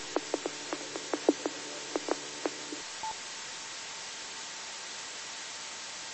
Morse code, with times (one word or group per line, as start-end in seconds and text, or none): none